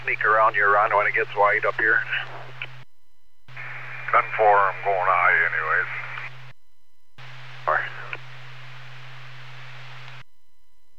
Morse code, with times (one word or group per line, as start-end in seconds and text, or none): none